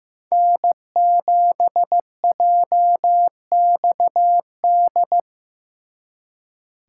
Morse code, heard as a keyed tone, about 18 wpm, 700 Hz. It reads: N7JXD
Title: Speed Racing - USA 1x3 Call Signs 15wpm